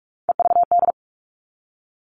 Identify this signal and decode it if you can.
Morse code, heard as a keyed tone, about 44 wpm, 700 Hz. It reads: EVD
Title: All Cut Numbers - 3-Digit 45wpm